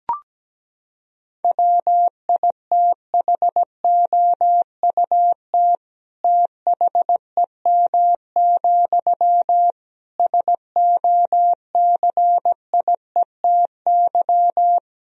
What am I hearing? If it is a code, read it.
Morse code, heard as a keyed tone, about 17 wpm, 700 Hz. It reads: WITHOUT THEM, SOCIETY
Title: Sentences from Top 1000 Words - Review Part 4 - 17wpm